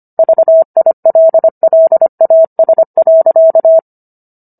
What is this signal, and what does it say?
Morse code, heard as a keyed tone, about 25 wpm, 650 Hz. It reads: VILLAS.